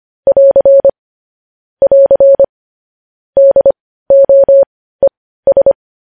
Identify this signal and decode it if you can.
Morse code, heard as a keyed tone, about 25 wpm, spaced as for 15 wpm, 550 Hz. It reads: <AR> <AR> DOES
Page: news-headlines-in-morse-code-at-15-wpm